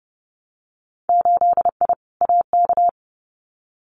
Morse code, heard as a keyed tone, about 30 wpm, 700 Hz. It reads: 8I AK
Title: Winter Field Day Exchanges Encore 30wpm